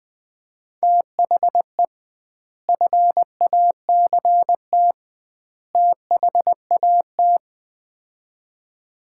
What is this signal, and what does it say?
Morse code, heard as a keyed tone, about 21 wpm, 700 Hz. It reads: THE FACT THAT
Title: 2x Word-Spacing - Sets of 3 Words - 20wpm